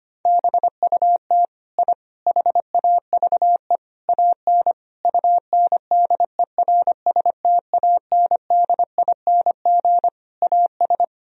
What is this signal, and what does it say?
Morse code, heard as a keyed tone, about 25 wpm, 700 Hz. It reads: BUT I HAVE AN UNDERSTANDING AS
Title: Sentences from Top 1000 Words - Review Part 3 - 25wpm